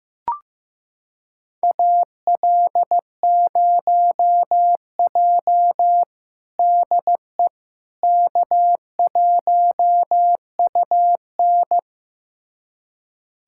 Morse code, heard as a keyed tone, about 15 wpm, 700 Hz. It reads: AL0J DE K1UN
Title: USA Callsign DE Callsign - 4 Characters - 15wpm